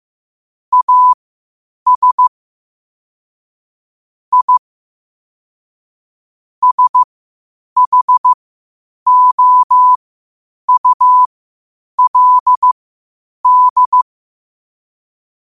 Morse code, heard as a keyed tone, about 15 wpm, spaced as for 8 wpm, 1000 Hz. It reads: AS I SHOULD
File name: qotdfiles / 8wpm